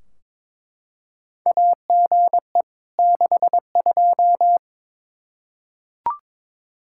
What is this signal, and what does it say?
Morse code, heard as a keyed tone, about 22 wpm, 700 Hz. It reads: AGE 62
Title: Practice QSO - Exchange 3 - More Info - 22wpm